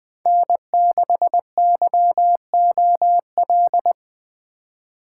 Morse code, heard as a keyed tone, about 20 wpm, 700 Hz. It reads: N6YOL